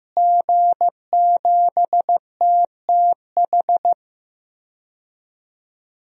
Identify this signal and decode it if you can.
Morse code, heard as a keyed tone, about 15 wpm, 700 Hz. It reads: G7TTH
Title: DXCC-100 Call Signs - 5 Characters - Repeated 3x - 15wpm